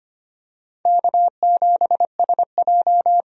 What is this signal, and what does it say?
Morse code, heard as a keyed tone, about 25 wpm, 700 Hz. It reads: K7SJ
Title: USA Call Signs - 4 Characters - 25wpm